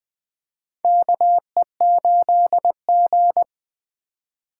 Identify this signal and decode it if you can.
Morse code, heard as a keyed tone, about 20 wpm, 700 Hz. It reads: KE8G
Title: USA Call Signs - 4x Character Spacing - 20wpm